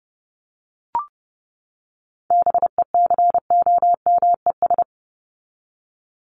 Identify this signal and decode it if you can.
Morse code, heard as a keyed tone, about 30 wpm, 700 Hz. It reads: BECOMES
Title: ES:S Suffix - Top 100 Words with Base Words 30wpm